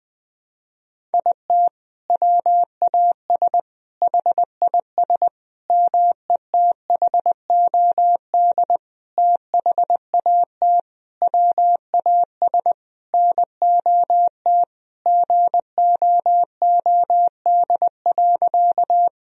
Morse code, heard as a keyed tone, about 21 wpm, 700 Hz. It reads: IT WAS HIS METHOD THAT WAS NOT GOOD.
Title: Sentences from Top 1000 Words - Review Part 3 - 20wpm